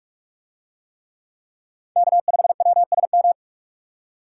Morse code, heard as a keyed tone, about 42 wpm, 700 Hz. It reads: K5WSM